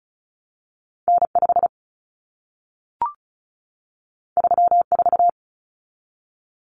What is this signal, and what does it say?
Morse code, heard as a keyed tone, about 36 wpm, 700 Hz. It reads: N5 34
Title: Common Cut Numbers - 2-Digit 35wpm